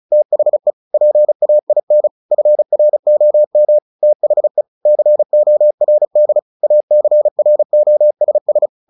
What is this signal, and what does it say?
Morse code, heard as a keyed tone, about 34 wpm, 600 Hz. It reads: THE PAIN FROM THE CORD ACROSS
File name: the old man and the sea 2 35wpm